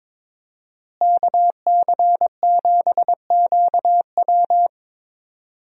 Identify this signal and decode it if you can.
Morse code, heard as a keyed tone, about 22 wpm, 700 Hz. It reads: KC7QW
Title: USA Call Signs - 5 Characters - Repeated 3x - 22wpm